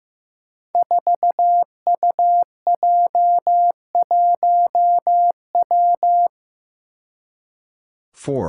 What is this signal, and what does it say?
Morse code, heard as a keyed tone, about 15 wpm, 700 Hz. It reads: 4UJ1W
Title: Five Challenging Letter-Number Combinations 15wpm